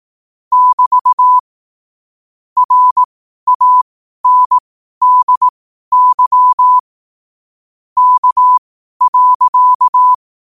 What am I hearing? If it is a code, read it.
Morse code, heard as a keyed tone, about 18 wpm, spaced as for 12 wpm, 1000 Hz. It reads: <BT> RANDY K.
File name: qotdfiles / 12wpm